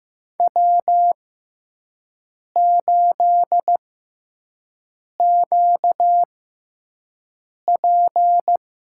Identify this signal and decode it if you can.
Morse code, heard as a keyed tone, about 15 wpm, spaced as for 5 wpm, 700 Hz. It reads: W8QP
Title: USA Call Signs - 4 Characters - 6x Character Spacing - 15wpm